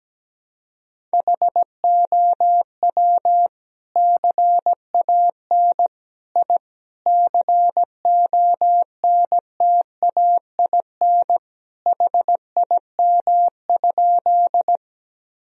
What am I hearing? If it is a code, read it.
Morse code, heard as a keyed tone, about 17 wpm, 700 Hz. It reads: HOW CAN I CONTAIN HIM?
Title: Sentences from Top 1000 Words - Review Part 2 - 17wpm